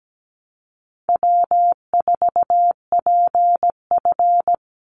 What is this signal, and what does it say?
Morse code, heard as a keyed tone, about 17 wpm, 700 Hz. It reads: W4PF
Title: Easy Call Signs - 17wpm